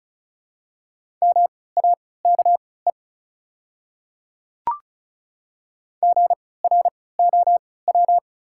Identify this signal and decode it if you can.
Morse code, heard as a keyed tone, about 34 wpm, spaced as for 20 wpm, 700 Hz. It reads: MAKE GROW
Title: 3x Character Spacing - Top 500 Words - 35wpm